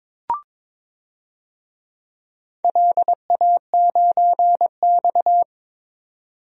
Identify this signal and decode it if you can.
Morse code, heard as a keyed tone, about 22 wpm, 700 Hz. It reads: LA9X
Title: DXCC-100 Call Signs - 4 Characters - 22wpm